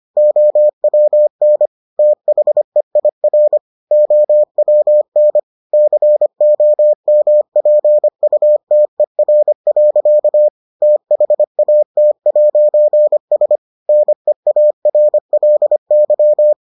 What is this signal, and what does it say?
Morse code, heard as a keyed tone, about 25 wpm, 600 Hz. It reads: OWN THEIR OWN COMPUTER. THAT'S NEARLY